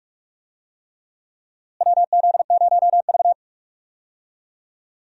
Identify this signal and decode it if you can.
Morse code, heard as a keyed tone, about 46 wpm, 700 Hz. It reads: WZ0V
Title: USA Call Signs - 4 Characters - Repeated 3x - 45wpm